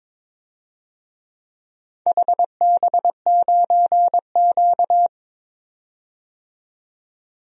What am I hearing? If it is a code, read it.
Morse code, heard as a keyed tone, about 23 wpm, 700 Hz. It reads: HB9Q
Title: DXCC-100 Call Signs - 4 Characters - 22wpm